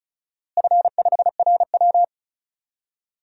Morse code, heard as a keyed tone, about 36 wpm, 700 Hz. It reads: F5RW
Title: DXCC-100 Call Signs - 4 Characters - Repeated 3x - 35wpm